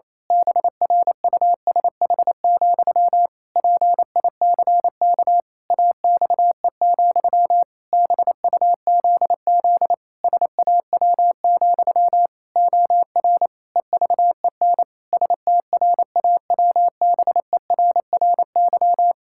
Morse code, heard as a keyed tone, about 28 wpm, 700 Hz. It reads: BRUSH, PICK AXE, BUZZ SAW, OR EVEN STRAWBERRY
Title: Pangram Challenge v2 - 28wpm